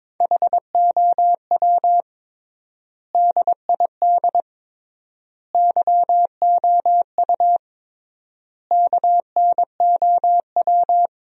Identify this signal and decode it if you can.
Morse code, heard as a keyed tone, about 22 wpm, 700 Hz. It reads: HOW DID YOU KNOW
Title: 3x Word-Spacing - Sentences from Top 100 Words - 22wpm